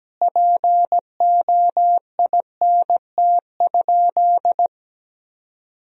Morse code, heard as a keyed tone, about 17 wpm, 700 Hz. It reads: POINT?
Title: Sentences from Top 1000 Words - Review Part 1 - 17wpm